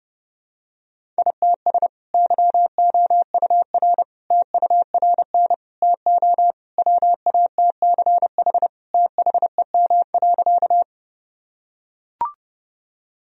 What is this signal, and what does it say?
Morse code, heard as a keyed tone, about 30 wpm, 700 Hz. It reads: ITS YOUR TURN TO WATCH THEM.
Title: Sentences from Top 1000 Words - Review Part 1 - 30wpm